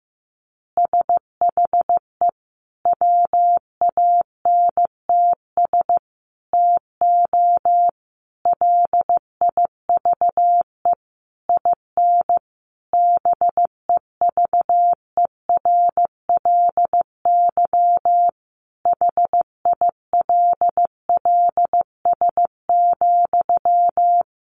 Morse code, heard as a keyed tone, about 15 wpm, 700 Hz. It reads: SHE WANTS TO LIVE IN BEVERLY HILLS,
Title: Easy Phrases 15wpm